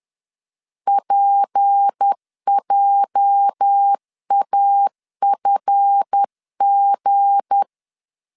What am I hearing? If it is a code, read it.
Morse code, heard as a keyed tone, about 10 wpm, 800 Hz. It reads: PJAFG
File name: LEZIONI / 9 2 10wpm 2736